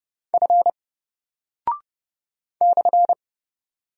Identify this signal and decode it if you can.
Morse code, heard as a keyed tone, about 30 wpm, 700 Hz. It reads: F /
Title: Single Character (Letter:Number:Punctuation:Prosign) 30wpm